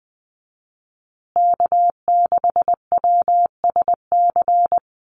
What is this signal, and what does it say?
Morse code, heard as a keyed tone, about 20 wpm, 700 Hz. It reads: K6WSC
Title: USA Call Signs Repeated 2x 20wpm